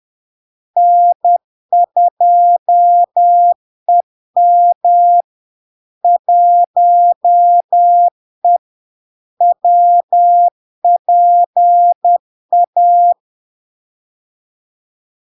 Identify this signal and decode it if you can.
Morse code, heard as a keyed tone, about 10 wpm, 700 Hz. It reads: N2EM 1E WPA
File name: LARK-FD-2022-Log-in-CW-10wpm-700-Hz